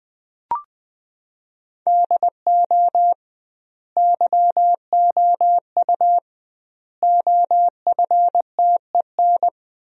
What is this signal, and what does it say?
Morse code, heard as a keyed tone, about 20 wpm, 700 Hz. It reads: DO YOU OFTEN
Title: 2x Word-Spacing - Sentences from Top 300 Words - 20wpm